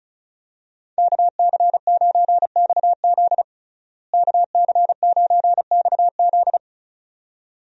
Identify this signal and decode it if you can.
Morse code, heard as a keyed tone, about 34 wpm, 700 Hz. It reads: KC9XZ KC9XZ
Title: USA Call Signs - 5 Characters - Repeated 2x - 35wpm